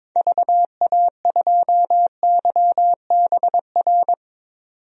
Morse code, heard as a keyed tone, about 22 wpm, 700 Hz. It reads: VA2YBR
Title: DXCC-100 Call Signs - 6 Characters - 22wpm